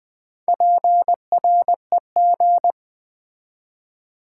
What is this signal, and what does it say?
Morse code, heard as a keyed tone, about 20 wpm, 700 Hz. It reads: PREG
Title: Build A Word - 2 Letters At A Time - 8-Letter Words - 20wpm